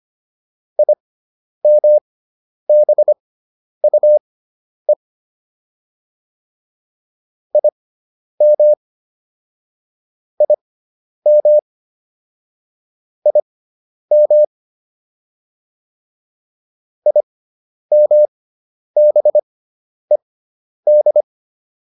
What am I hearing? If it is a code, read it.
Morse code, heard as a keyed tone, about 25 wpm, spaced as for 10 wpm, 600 Hz. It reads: IMBUE IM IM IM IMBED